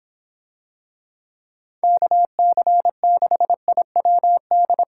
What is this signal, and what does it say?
Morse code, heard as a keyed tone, about 24 wpm, 700 Hz. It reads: KC6IWD